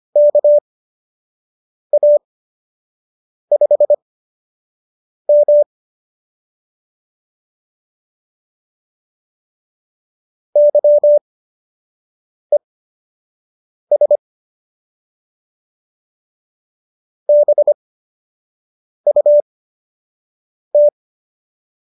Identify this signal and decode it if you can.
Morse code, heard as a keyed tone, about 25 wpm, spaced as for 6 wpm, 570 Hz. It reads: KA5M YES BUT